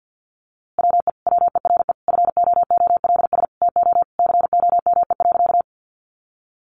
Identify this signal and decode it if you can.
Morse code, heard as a keyed tone, about 50 wpm, 700 Hz. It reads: WE WERE FOOLS TO COME.
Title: A Sound Of Thunder 50wpm